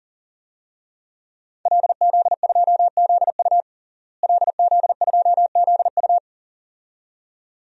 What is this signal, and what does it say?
Morse code, heard as a keyed tone, about 40 wpm, 700 Hz. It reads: LZ2ZU LZ2ZU
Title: DXCC-100 Call Signs - 5 Characters - Repeated 2x - 40wpm